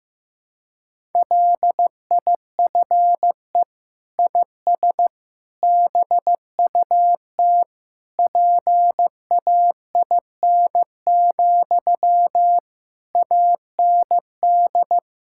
Morse code, heard as a keyed tone, about 15 wpm, 700 Hz. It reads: LIFE IS BUT PAIN, AND
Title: Sentences from Top 1600 Words 15wpm